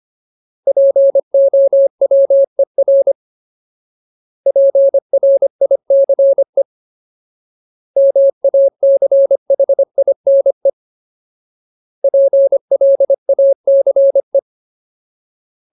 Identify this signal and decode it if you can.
Morse code, heard as a keyed tone, about 25 wpm, 550 Hz. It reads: POWER PRICE MACHINE PLACE
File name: Ultimate Word List / 25WPM Ultimate